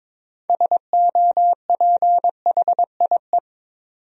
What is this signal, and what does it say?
Morse code, heard as a keyed tone, about 22 wpm, 700 Hz. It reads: SOPHIE
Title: USA Common Names - 22wpm